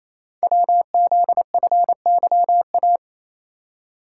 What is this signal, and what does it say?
Morse code, heard as a keyed tone, about 29 wpm, 700 Hz. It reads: WZFYA
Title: Code Groups - 5 Letters - 28wpm